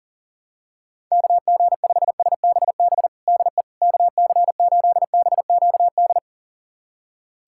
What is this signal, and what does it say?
Morse code, heard as a keyed tone, about 40 wpm, 700 Hz. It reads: KG5SBB DE KC8BQD
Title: USA Callsign DE Callsign - 6 Characters - 40wpm